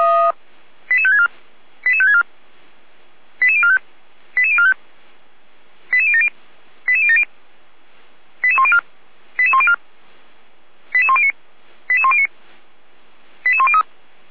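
Five-tone selective call (ZVEI), heard as quick runs of five tones, tones of about 70 ms. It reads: 88564 88564 88045 88045 88070 88070 88104 88104 88109 88109 88103
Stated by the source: ZVEI five-tone call received by SDR, logged by fire brigade Feuerwehr Orlamünde (f85360).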